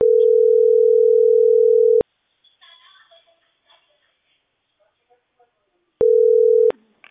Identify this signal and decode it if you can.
Ringback tone (North America), 440 + 480 Hz.